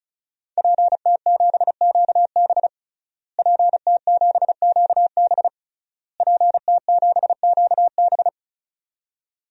Morse code, heard as a keyed tone, about 36 wpm, 700 Hz. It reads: PT7QB PT7QB PT7QB
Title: DXCC-100 Call Signs - 5 Characters - Repeated 3x - 35wpm